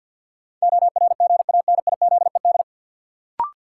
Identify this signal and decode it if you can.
Morse code, heard as a keyed tone, about 53 wpm, 700 Hz. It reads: ORGANIZED
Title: Sentences from Top 1300 Words - New Word Drill Alt - 50wpm